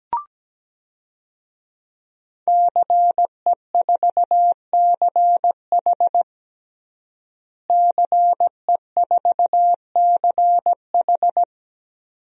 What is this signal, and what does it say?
Morse code, heard as a keyed tone, about 17 wpm, 700 Hz. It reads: CE4CH CE4CH
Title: DXCC-100 Call Signs - 5 Characters - Repeated 2x - 17wpm